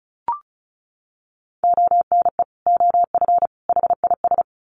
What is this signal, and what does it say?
Morse code, heard as a keyed tone, about 34 wpm, 700 Hz. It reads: ONE OF HIS
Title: Sets of 3 Words 35wpm